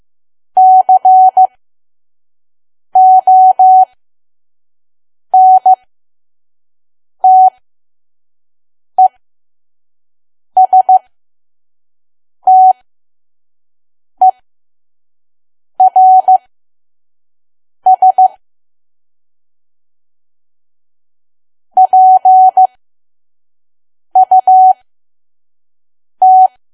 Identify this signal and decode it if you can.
Morse code, heard as a keyed tone, about 15 wpm, spaced as for 5 wpm, 750 Hz. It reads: CONTESTERS PUT
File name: Archive / 5 WPM files / PM